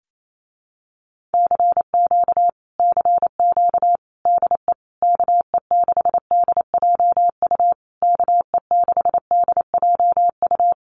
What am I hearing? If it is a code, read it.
Morse code, heard as a keyed tone, about 28 wpm, 700 Hz. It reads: CQ CQ DE KE6DJU KE6DJU
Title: Common QSO Phrases Encore 28wpm